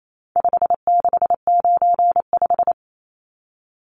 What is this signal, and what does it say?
Morse code, heard as a keyed tone, about 28 wpm, 700 Hz. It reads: 5695
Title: Numbers - 4-Digit 28wpm